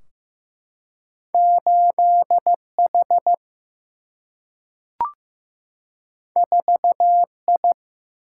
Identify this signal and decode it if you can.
Morse code, heard as a keyed tone, about 15 wpm, 700 Hz. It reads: 8 H 4 I
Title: Single-Letter Single-Number 15wpm